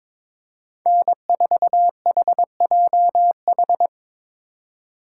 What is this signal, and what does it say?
Morse code, heard as a keyed tone, about 25 wpm, 700 Hz. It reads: N4HJH
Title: Speed Racing - USA 1x3 Call Signs 22wpm